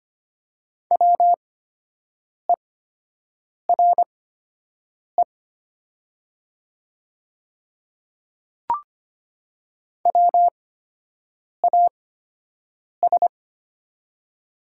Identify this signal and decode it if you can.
Morse code, heard as a keyed tone, about 25 wpm, spaced as for 7 wpm, 690 Hz. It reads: WERE WAS
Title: 8x Character Spacing - Top 300 Words - 25wpm